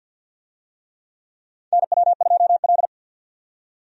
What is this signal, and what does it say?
Morse code, heard as a keyed tone, about 52 wpm, 700 Hz. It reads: NW2L
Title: USA Call Signs - 4 Characters - 50wpm